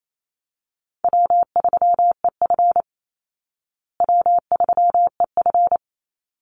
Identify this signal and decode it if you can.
Morse code, heard as a keyed tone, about 28 wpm, 700 Hz. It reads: W3EF W3EF
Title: Easy Call Signs Repeated 3x - 28wpm